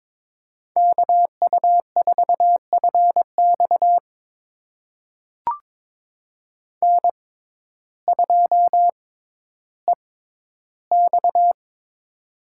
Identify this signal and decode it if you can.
Morse code, heard as a keyed tone, about 22 wpm, spaced as for 12 wpm, 700 Hz. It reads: KU4FX N2EX
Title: USA Call Signs - 6x Character Spacing - 22wpm